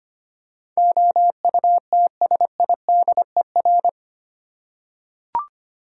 Morse code, heard as a keyed tone, about 25 wpm, 700 Hz. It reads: OUTSIDER
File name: Eight Letter Words - 25wpm